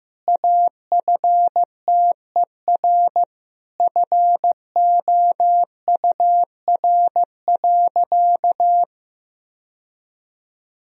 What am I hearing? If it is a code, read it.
Morse code, heard as a keyed tone, about 15 wpm, 700 Hz. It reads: AFTER FOUR.
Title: Sentences from Top 1000 Words - Review Part 1 - 15wpm